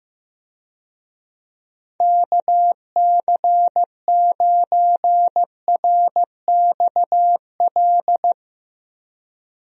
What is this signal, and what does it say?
Morse code, heard as a keyed tone, about 15 wpm, 700 Hz. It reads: KC9RXL
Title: USA Call Signs - 6 Characters - 15wpm